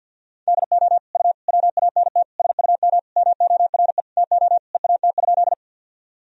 Morse code, heard as a keyed tone, about 51 wpm, 700 Hz. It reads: DO U WANT SUM MORE TO EAT?
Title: Sentences from Top 300 Words and Abbreviations 50wpm